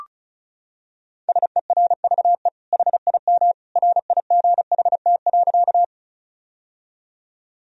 Morse code, heard as a keyed tone, about 36 wpm, 700 Hz. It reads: SERVE HIM RIGHT.
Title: Sentences from Top 1000 Words - Review Part 2 - 35wpm